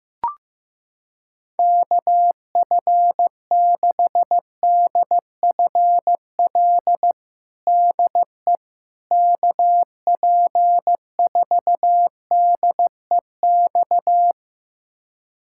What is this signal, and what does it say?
Morse code, heard as a keyed tone, about 15 wpm, 700 Hz. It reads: KF6DFL DE KP4DEX
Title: USA Callsign DE Callsign - 6 Characters - 15wpm